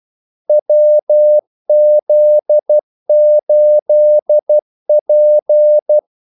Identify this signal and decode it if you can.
Morse code, heard as a keyed tone, about 12 wpm, 600 Hz. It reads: WZ8P
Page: 2 ekstra ordmellemrum mellem kaldesignalerne